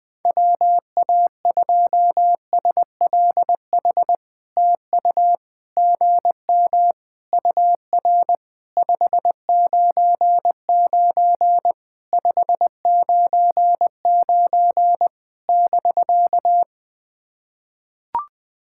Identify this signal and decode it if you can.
Morse code, heard as a keyed tone, about 20 wpm, 700 Hz. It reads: WA2SLH TU GM UR 599 599 <BK>
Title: POTA QSO Practice - 20wpm